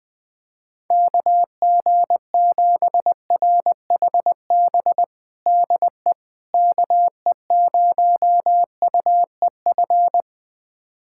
Morse code, heard as a keyed tone, about 20 wpm, 700 Hz. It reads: KG7RHB DE KE0UEF
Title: USA Callsign DE Callsign - 6 Characters - 20wpm